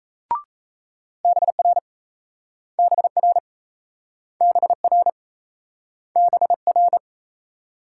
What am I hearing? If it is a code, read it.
Morse code, heard as a keyed tone, about 38 wpm, 700 Hz. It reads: BR BR BR BR